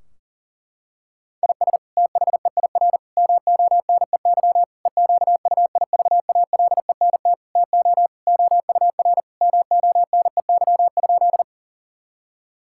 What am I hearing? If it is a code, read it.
Morse code, heard as a keyed tone, about 40 wpm, 700 Hz. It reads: IS THEIR MONEY EQUIVALENT TO OUR MONEY?
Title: Sentences from Top 1600 Words 40wpm